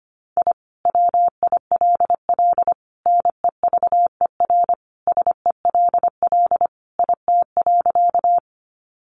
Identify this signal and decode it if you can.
Morse code, heard as a keyed tone, about 25 wpm, 700 Hz. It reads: I WILL NEVER SELL IT.
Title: Sentences from Top 1100 Words 25wpm